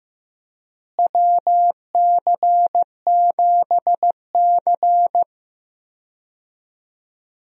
Morse code, heard as a keyed tone, about 15 wpm, 700 Hz. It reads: WC7C
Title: USA Call Signs - 4 Characters - Repeated 3x - 15wpm